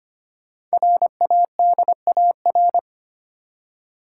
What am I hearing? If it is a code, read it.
Morse code, heard as a keyed tone, about 25 wpm, 700 Hz. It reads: RADAR
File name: 5 Letter Words - First 3 Letters Given - 25wpm